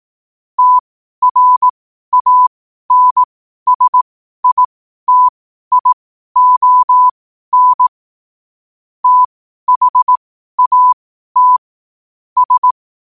Morse code, heard as a keyed tone, about 18 wpm, spaced as for 12 wpm, 1000 Hz. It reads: TRANSITION THAT S
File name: qotdfiles / 12wpm